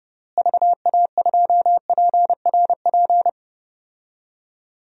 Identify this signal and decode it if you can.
Morse code, heard as a keyed tone, about 30 wpm, 700 Hz. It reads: VA2PRP
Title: DXCC-100 Call Signs - 6 Characters - Repeated 2x - 30wpm